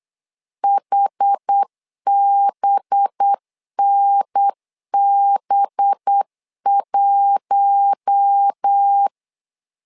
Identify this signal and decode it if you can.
Morse code, heard as a keyed tone, about 8 wpm, 800 Hz. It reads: HBNB1
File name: LEZIONI / 5 LBVC / 5 LB 8wpm 2816